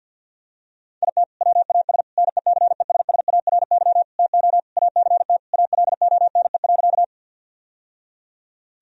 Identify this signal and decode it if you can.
Morse code, heard as a keyed tone, about 50 wpm, 700 Hz. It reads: IT WAS NECESSARY TO ACT ALONE.
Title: Sentences from Top 1000 Words - Review Part 1 - 50wpm